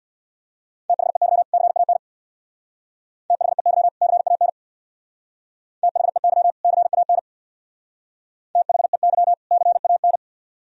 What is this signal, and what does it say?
Morse code, heard as a keyed tone, about 63 wpm, 700 Hz. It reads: THEY CAN THEY CAN THEY CAN THEY CAN